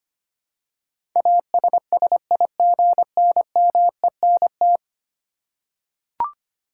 Morse code, heard as a keyed tone, about 25 wpm, 700 Hz. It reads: ASSIGNMENT
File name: Ten Letter Words - 25wpm